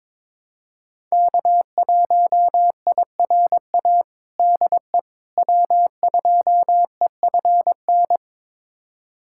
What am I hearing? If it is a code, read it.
Morse code, heard as a keyed tone, about 22 wpm, 700 Hz. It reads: K1IRA DE W2EFN
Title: USA Callsign DE Callsign - 5 Characters - 22wpm